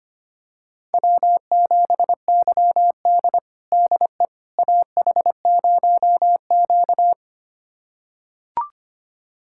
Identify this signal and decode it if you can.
Morse code, heard as a keyed tone, about 25 wpm, 700 Hz. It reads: W7YD DE AH0Q
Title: USA Callsign DE Callsign - 4 Characters - 25wpm